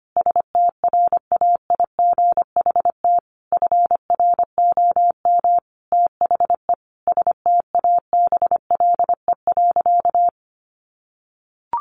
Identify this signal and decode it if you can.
Morse code, heard as a keyed tone, about 25 wpm, 700 Hz. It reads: STRAIGHT FROM THE STABLE.
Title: Sentences from Top 1500 Words 25wpm